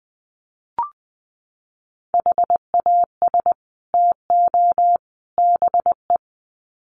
Morse code, heard as a keyed tone, about 20 wpm, 700 Hz. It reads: HAS TO BE
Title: Sets of 3 Words 20wpm